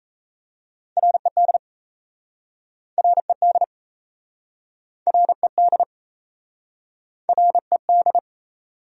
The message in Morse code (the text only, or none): RED RED RED RED